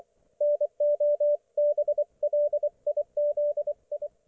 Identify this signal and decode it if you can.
Morse code, heard as a keyed tone, about 24 wpm, 590 Hz, with radio noise. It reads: NO BLIZI
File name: Archiv / CUC212 / ok1fhi cul212